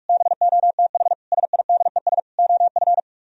Morse code, heard as a keyed tone, about 61 wpm, spaced as for 53 wpm, 700 Hz. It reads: BOTH SIDES OF
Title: Speed-Racing - Sets of 4 Words 45wpm